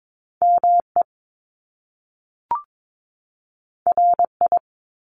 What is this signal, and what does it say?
Morse code, heard as a keyed tone, about 22 wpm, 700 Hz. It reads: ME RI
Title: Top Two Letter Combinations in Words 22wpm